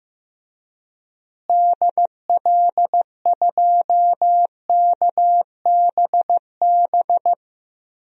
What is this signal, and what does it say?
Morse code, heard as a keyed tone, about 15 wpm, 700 Hz. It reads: DL2KBB
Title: DXCC-100 Call Signs - 6 Characters - Repeated 3x - 15wpm